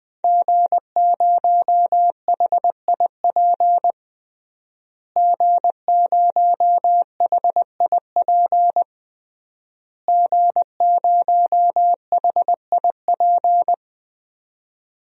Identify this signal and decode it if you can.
Morse code, heard as a keyed tone, about 20 wpm, 700 Hz. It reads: G0HIP G0HIP G0HIP
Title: DXCC-100 Call Signs - 5 Characters - Repeated 3x - 20wpm